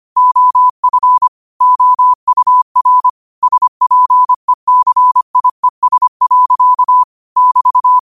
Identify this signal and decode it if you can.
Morse code, heard as a keyed tone, about 25 wpm, 1000 Hz. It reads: OF OUR SPECIES. <BT>